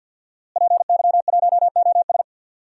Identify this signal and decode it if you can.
Morse code, heard as a keyed tone, about 50 wpm, 700 Hz. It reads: PY1OS
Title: DXCC-100 Call Signs - 5 Characters - Repeated 3x - 50wpm